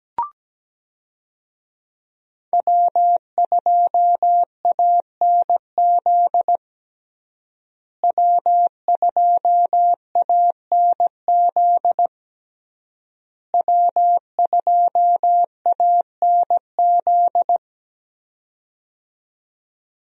Morse code, heard as a keyed tone, about 17 wpm, 700 Hz. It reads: W2ANZ W2ANZ W2ANZ
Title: USA Call Signs - 5 Characters - Repeated 3x - 17wpm